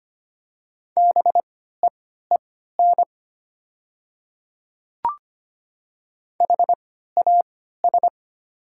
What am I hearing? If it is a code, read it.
Morse code, heard as a keyed tone, about 25 wpm, spaced as for 14 wpm, 700 Hz. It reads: BEEN HAS
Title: 3x Character Spacing - Top 200 Words - 25wpm